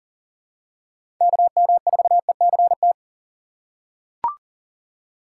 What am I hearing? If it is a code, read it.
Morse code, heard as a keyed tone, about 39 wpm, 700 Hz. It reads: KM4ECT
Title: USA Call Signs - 3x Character Spacing - 40wpm